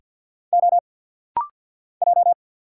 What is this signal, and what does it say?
Morse code, heard as a keyed tone, about 50 wpm, 700 Hz. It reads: O J
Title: Single Letters - Rapid-Fire 50wpm